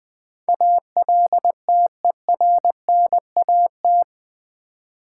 Morse code, heard as a keyed tone, about 20 wpm, 700 Hz. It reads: ALTERNAT
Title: Build A Word - 4 Letters At A Time - 11-Letter Words - 20wpm